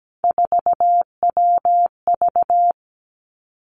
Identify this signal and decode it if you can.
Morse code, heard as a keyed tone, about 17 wpm, 700 Hz. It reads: 4WV
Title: Three Challenging Letter-Number Combinations 17wpm